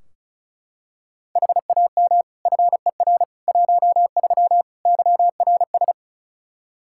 Morse code, heard as a keyed tone, about 36 wpm, 700 Hz. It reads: HAM FER 13 YRS
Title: Practice QSO - Exchange 3 - More Info - 35wpm